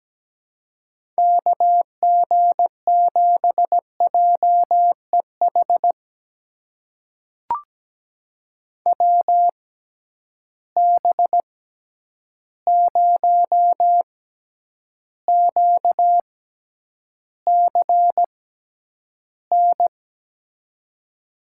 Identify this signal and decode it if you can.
Morse code, heard as a keyed tone, about 17 wpm, spaced as for 9 wpm, 700 Hz. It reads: KG7JEH WB0QCN
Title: USA Call Signs - 6 Characters - 6x Character Spacing - 17wpm